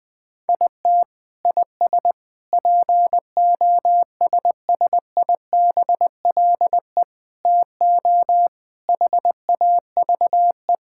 Morse code, heard as a keyed tone, about 21 wpm, 700 Hz. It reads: IT IS POSSIBLE TO HAVE
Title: Sentences from Top 1600 Words 20wpm